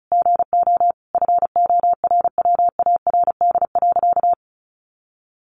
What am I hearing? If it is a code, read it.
Morse code, heard as a keyed tone, about 34 wpm, 700 Hz. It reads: GO FORWARD.
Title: Sentences from Top 1300 Words 35wpm